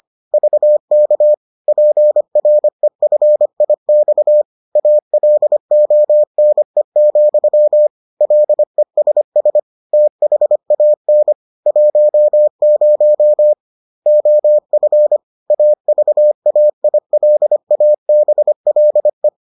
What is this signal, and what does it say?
Morse code, heard as a keyed tone, about 25 wpm, 600 Hz. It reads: VK PREFIX ALONE, LESS THAN 10 OF AVAILABLE